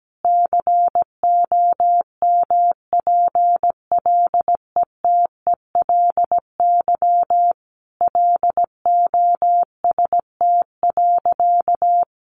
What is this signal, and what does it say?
Morse code, heard as a keyed tone, about 17 wpm, 700 Hz. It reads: COMPLETELY LOST.
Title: Additional Easy Phrases 17wpm